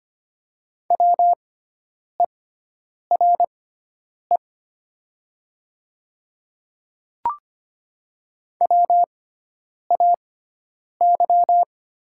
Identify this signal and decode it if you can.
Morse code, heard as a keyed tone, about 25 wpm, spaced as for 9 wpm, 700 Hz. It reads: WERE WAY
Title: 6x Character Spacing - Top 100 Words - 25wpm